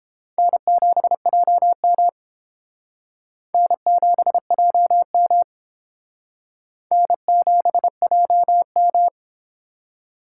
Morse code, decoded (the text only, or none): N7JM N7JM N7JM